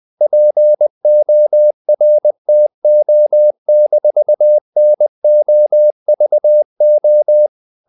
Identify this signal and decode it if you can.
Morse code, heard as a keyed tone, about 20 wpm, 600 Hz. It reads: PORTO-NOVO
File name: CW / Practice / NationalCapitals20